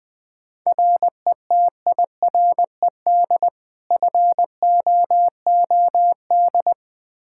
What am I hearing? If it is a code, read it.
Morse code, heard as a keyed tone, about 20 wpm, 700 Hz. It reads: RETIRED FOOD
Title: Practice QSO - Exchange 3 - More Info - 20wpm